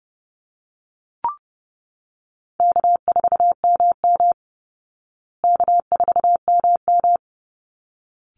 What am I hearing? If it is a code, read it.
Morse code, heard as a keyed tone, about 30 wpm, 700 Hz. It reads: K4MM K4MM
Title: Call Signs Repeated 3x - 30wpm